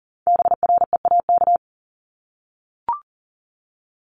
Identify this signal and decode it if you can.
Morse code, heard as a keyed tone, about 42 wpm, 700 Hz. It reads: BREAK
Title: Multi-letter Phonemes 40wpm